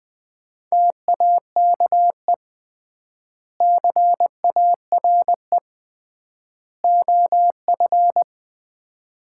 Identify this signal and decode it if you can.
Morse code, heard as a keyed tone, about 20 wpm, 700 Hz. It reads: TAKE CARE OF
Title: 3x Word-Spacing - Sets of 3 Words - 20wpm